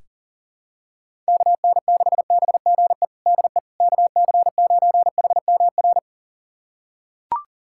Morse code, heard as a keyed tone, about 37 wpm, 700 Hz. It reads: KN6BGE DE KC9HMR
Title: USA Callsign DE Callsign 40wpm